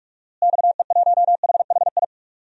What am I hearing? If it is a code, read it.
Morse code, heard as a keyed tone, about 42 wpm, 700 Hz. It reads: XE1HHI
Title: DXCC-100 Call Signs - 6 Characters - Repeated 3x - 45wpm